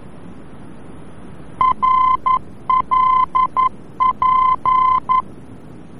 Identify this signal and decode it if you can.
Morse code, heard as a keyed tone, about 11 wpm, 1030 Hz. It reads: RLP
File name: france / mx / RLP-VOR-F